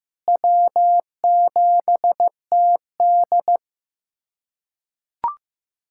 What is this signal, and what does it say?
Morse code, heard as a keyed tone, about 15 wpm, 700 Hz. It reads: W7TD
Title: USA Callsign DE Callsign - 4 Characters - 15wpm